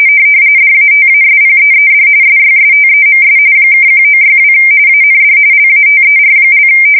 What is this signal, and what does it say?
RTTY, about 45.45 baud. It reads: DATASPEED SYSTEMS (1962 AND LATER)
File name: fsk170 / teletype-corp